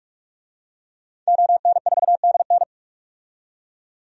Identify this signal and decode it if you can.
Morse code, heard as a keyed tone, about 47 wpm, 700 Hz. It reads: ON4DN
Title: DXCC-100 Call Signs - 5 Characters - 45wpm